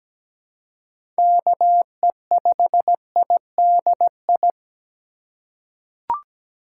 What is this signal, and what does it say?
Morse code, heard as a keyed tone, about 17 wpm, 700 Hz. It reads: KE5IDI
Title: USA Call Signs - 6 Characters - 2x Character Spacing - 17wpm